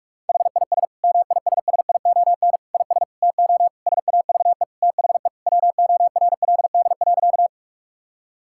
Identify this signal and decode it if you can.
Morse code, heard as a keyed tone, about 41 wpm, 700 Hz. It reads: HIS MISSION IS TO SAVE THE WORLD.
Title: Sentences from Top 1600 Words 45wpm